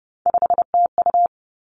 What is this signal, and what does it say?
Morse code, heard as a keyed tone, about 30 wpm, 700 Hz. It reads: 5TU
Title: Three Challenging Letter-Number Combinations 30wpm